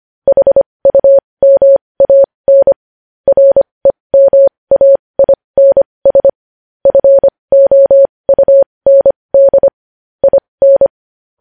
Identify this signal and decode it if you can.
Morse code, heard as a keyed tone, about 25 wpm, spaced as for 20 wpm, 550 Hz. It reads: HUMAN REMAINS FOUND IN